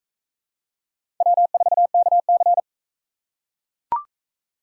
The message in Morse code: W4KC

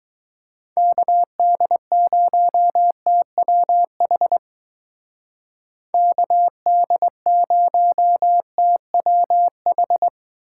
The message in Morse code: KD0TWH KD0TWH

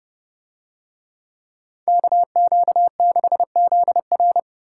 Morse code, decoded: KQ6ZR

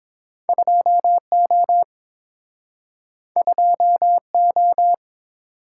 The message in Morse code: 2O 2O

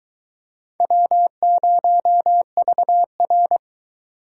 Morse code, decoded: W0VR

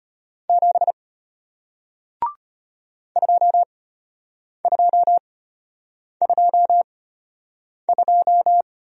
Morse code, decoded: 7 2222

